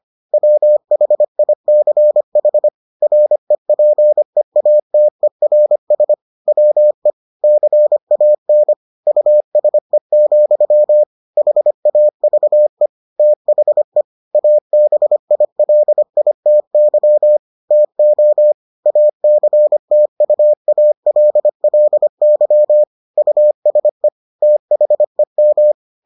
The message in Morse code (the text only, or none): WHICH REPEATERS WE CAN USE, HAVE THE ABILITY TO ACTUALLY USE THEM